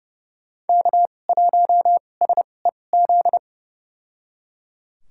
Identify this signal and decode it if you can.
Morse code, heard as a keyed tone, about 30 wpm, spaced as for 22 wpm, 700 Hz. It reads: K1SEZ
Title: USA Call Signs - 5 Characters - 2x Character Spacing - 30wpm